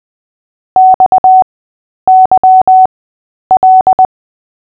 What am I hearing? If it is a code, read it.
Morse code, heard as a keyed tone, about 20 wpm, spaced as for 10 wpm, 750 Hz. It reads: XYL
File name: Short-Story-5-10wpm